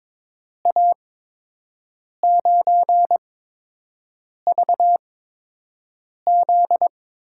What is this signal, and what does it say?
Morse code, heard as a keyed tone, about 22 wpm, spaced as for 6 wpm, 700 Hz. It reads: A9VZ
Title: USA Call Signs - 8x Character Spacing - 22wpm